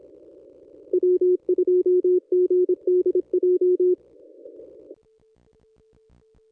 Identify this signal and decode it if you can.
Morse code, heard as a keyed tone, about 25 wpm, 360 Hz. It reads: W2GDJ
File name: contests / ss-cw-k5tr-op-n5rz -1000